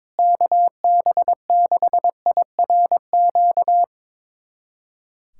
Morse code, heard as a keyed tone, about 22 wpm, 700 Hz. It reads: KB6IRQ
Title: USA Callsign DE Callsign 22wpm